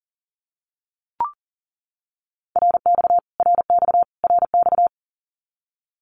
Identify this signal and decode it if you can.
Morse code, heard as a keyed tone, about 40 wpm, 700 Hz. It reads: RX RX RX
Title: Abbreviations Repeated 3x - 40wpm